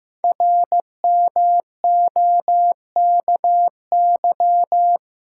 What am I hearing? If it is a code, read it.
Morse code, heard as a keyed tone, about 15 wpm, 700 Hz. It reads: RMOKY
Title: Code Groups - 5 Letters - 15wpm